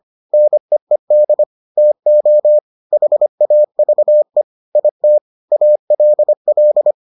Morse code, heard as a keyed tone, about 25 wpm, 600 Hz. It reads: NEED TO HAVE IT ALL